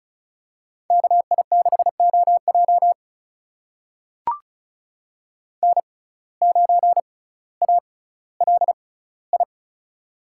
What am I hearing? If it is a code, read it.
Morse code, heard as a keyed tone, about 36 wpm, spaced as for 18 wpm, 700 Hz. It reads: KI6OJ N9ALI
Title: USA Call Signs - 5 Characters - 6x Character Spacing - 35wpm